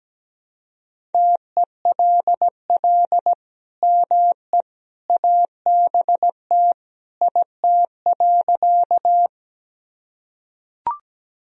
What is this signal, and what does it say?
Morse code, heard as a keyed tone, about 17 wpm, 700 Hz. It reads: TELL ME ABT IT.
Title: Sentences from Top 200 Words and Abbreviations 17wpm